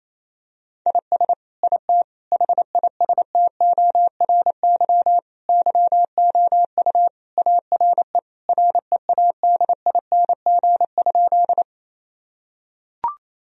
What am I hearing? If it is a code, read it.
Morse code, heard as a keyed tone, about 27 wpm, 700 Hz. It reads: IS IT HISTORY YOU ARE READING?
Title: Sentences from Top 1000 Words - Review Part 3 - 28wpm